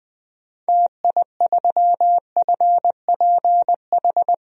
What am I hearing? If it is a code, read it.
Morse code, heard as a keyed tone, about 20 wpm, 700 Hz. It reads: TI3FPH
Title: DXCC-100 Call Signs - 6 Characters - 20wpm